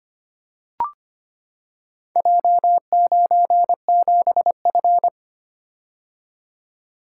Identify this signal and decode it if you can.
Morse code, heard as a keyed tone, about 25 wpm, 700 Hz. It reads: J97F
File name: practice / 25wpm / Four Letter-Number Combinations - 25wpm